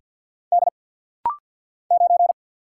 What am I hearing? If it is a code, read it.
Morse code, heard as a keyed tone, about 49 wpm, 700 Hz. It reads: D 9
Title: Single Letter-Number - Rapid-Fire 50wpm